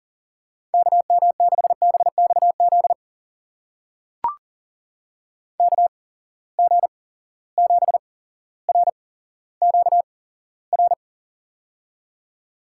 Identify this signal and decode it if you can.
Morse code, heard as a keyed tone, about 39 wpm, spaced as for 17 wpm, 700 Hz. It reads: KM6BXZ KG7RQR
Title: USA Call Signs - 6 Characters - 8x Character Spacing - 40wpm